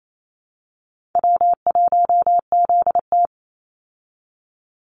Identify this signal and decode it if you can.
Morse code, heard as a keyed tone, about 28 wpm, 700 Hz. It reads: W1ZT
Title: Call Signs Repeated 2x - 28wpm